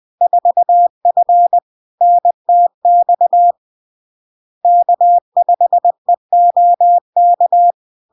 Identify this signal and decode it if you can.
Morse code, heard as a keyed tone, about 20 wpm, 700 Hz. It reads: 4F NTX K5EOK
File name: LARK-FD-2022-Log-in-CW-20wpm-700-Hz